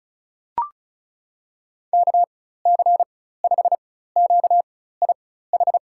Morse code, heard as a keyed tone, about 34 wpm, spaced as for 16 wpm, 700 Hz. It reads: KC5QIH
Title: USA Call Signs - 6 Characters - 4x Character Spacing - 35wpm